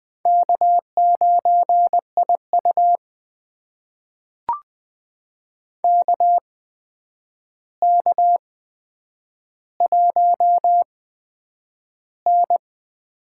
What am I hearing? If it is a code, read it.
Morse code, heard as a keyed tone, about 20 wpm, spaced as for 9 wpm, 700 Hz. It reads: K9IU KK1N
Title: USA Call Signs - 4 Characters - 8x Character Spacing - 20wpm